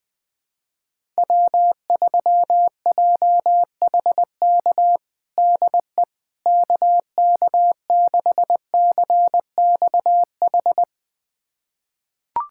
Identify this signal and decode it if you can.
Morse code, heard as a keyed tone, about 20 wpm, 700 Hz. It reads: W3JHK DE KK6CXH
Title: USA Callsign DE Callsign 20wpm